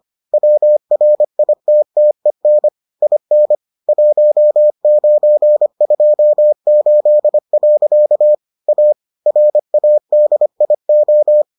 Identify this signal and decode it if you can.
Morse code, heard as a keyed tone, about 25 wpm, 600 Hz. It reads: WRITTEN IN 1928. A RADIO